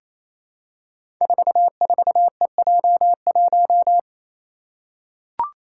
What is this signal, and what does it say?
Morse code, heard as a keyed tone, about 28 wpm, 700 Hz. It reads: 44EJ1
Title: Code Groups - 5 Letter:Numbers - 28wpm